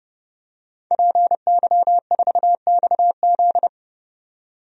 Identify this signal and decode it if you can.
Morse code, heard as a keyed tone, about 30 wpm, 700 Hz. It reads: PY4XZ
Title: DXCC-100 Call Signs - 5 Characters - 30wpm